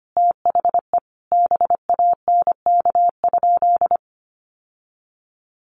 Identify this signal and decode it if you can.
Morse code, heard as a keyed tone, about 25 wpm, 700 Hz. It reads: THE BANK?
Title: Sentences from Top 1200 Words 25wpm